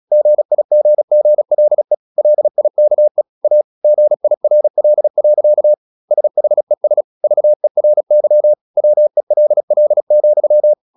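Morse code, heard as a keyed tone, about 36 wpm, 600 Hz. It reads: GIGGLE LIKE A GIRL. SHES VERY WELL,
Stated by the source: Morse practice at 36 wpm, text of Ken Follett's novel Never, from CW med Gnister.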